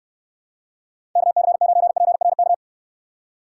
Morse code, heard as a keyed tone, about 64 wpm, 700 Hz. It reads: KC9PRK